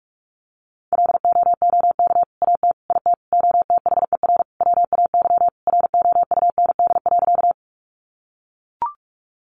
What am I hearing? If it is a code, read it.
Morse code, heard as a keyed tone, about 44 wpm, 700 Hz. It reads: LOOK AT IT OTHER WAY ROUND.
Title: Sentences from Top 200 Words 45wpm